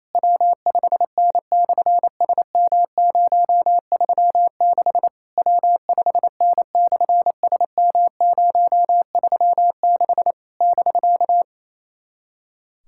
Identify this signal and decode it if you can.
Morse code, heard as a keyed tone, about 28 wpm, 700 Hz. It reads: W5N/SM036 W5N/SM036 <BK>
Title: SOTA QSO Practice 28wpm